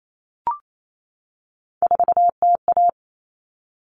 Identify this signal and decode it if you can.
Morse code, heard as a keyed tone, about 28 wpm, 700 Hz. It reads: 4TA
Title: All Cut Numbers - 3-Digit 28wpm